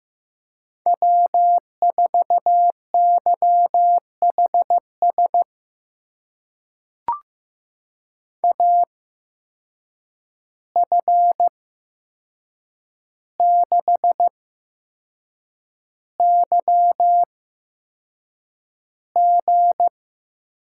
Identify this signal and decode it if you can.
Morse code, heard as a keyed tone, about 15 wpm, spaced as for 6 wpm, 700 Hz. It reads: W4YHS AF6YG
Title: USA Call Signs - 5 Characters - 8x Character Spacing - 15wpm